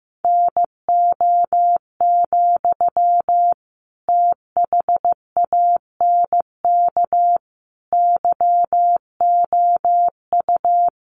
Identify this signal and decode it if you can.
Morse code, heard as a keyed tone, about 15 wpm, 700 Hz. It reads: NO, THANK YOU
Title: Sentences from Top 600 Words 15wpm